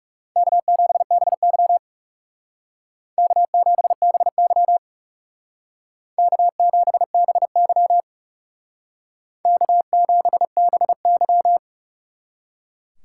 Morse code, heard as a keyed tone, about 38 wpm, 700 Hz. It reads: K7BY K7BY K7BY K7BY